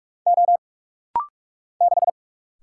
Morse code, heard as a keyed tone, about 44 wpm, 700 Hz. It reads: O 6
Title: Single Letter-Number - Rapid-Fire 45wpm